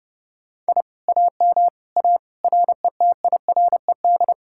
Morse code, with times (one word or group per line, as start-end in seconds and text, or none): 0.7-0.8 I
1.1-1.7 AM
2.0-2.2 A
2.4-4.3 RETIRED